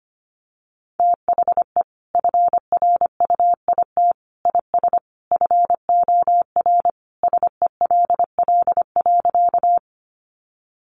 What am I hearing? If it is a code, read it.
Morse code, heard as a keyed tone, about 25 wpm, 700 Hz. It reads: THE FRUIT IS FOR SELL.
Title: Sentences from Top 1300 Words 25wpm